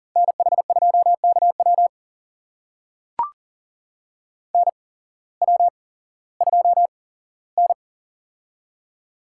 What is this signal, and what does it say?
Morse code, heard as a keyed tone, about 39 wpm, spaced as for 19 wpm, 700 Hz. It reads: NH2KW NW2N
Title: USA Call Signs - 8x Character Spacing - 40wpm